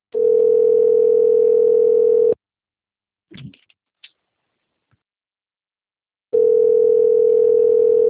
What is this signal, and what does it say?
Ringback tone (North America), 440 + 480 Hz.